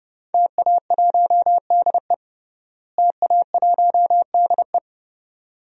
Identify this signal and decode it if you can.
Morse code, heard as a keyed tone, about 30 wpm, 700 Hz. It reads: TA1DE TA1DE
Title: DXCC-100 Call Signs - 5 Characters - Repeated 2x - 30wpm